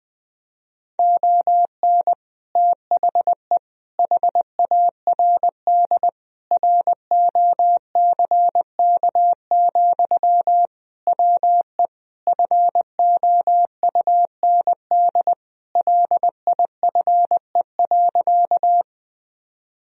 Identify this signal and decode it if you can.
Morse code, heard as a keyed tone, about 20 wpm, 700 Hz. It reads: ON THE HARD ROCK, WE FOUND LIFE.
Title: Sentences from Top 1000 Words - Review Part 2 - 20wpm